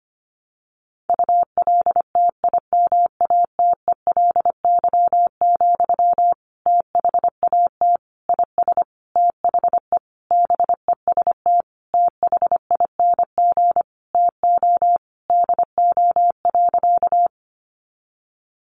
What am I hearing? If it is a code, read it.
Morse code, heard as a keyed tone, about 25 wpm, 700 Hz. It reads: ULTIMATELY, THAT IS THE BEST THING TO DO.
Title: Sentences from Top 1200 Words 25wpm